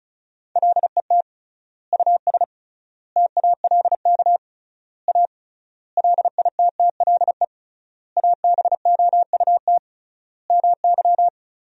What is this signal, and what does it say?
Morse code, heard as a keyed tone, about 34 wpm, 700 Hz. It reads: LET US TALK A LITTLE ABOUT MY
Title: 3x Word-Spacing - Sentences from Top 400 Words - 35wpm